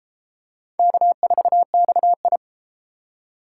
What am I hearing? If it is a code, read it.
Morse code, heard as a keyed tone, about 33 wpm, 700 Hz. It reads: K4XI